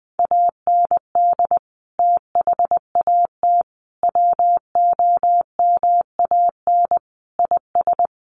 Morse code, heard as a keyed tone, about 20 wpm, 700 Hz. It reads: AND THAT WOMAN IS
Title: Sentences from Top 1100 Words 20wpm